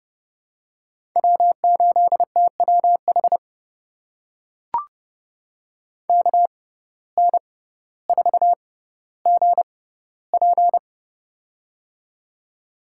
Morse code, heard as a keyed tone, about 30 wpm, spaced as for 15 wpm, 700 Hz. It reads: W8TWH KN4GP
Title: USA Call Signs - 5 Characters - 6x Character Spacing - 30wpm